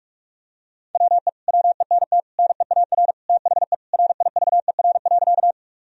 Morse code, heard as a keyed tone, about 42 wpm, 700 Hz. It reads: WE WENT NEAR THE RIVER.
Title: Sentences from Top 1000 Words - Review Part 1 - 45wpm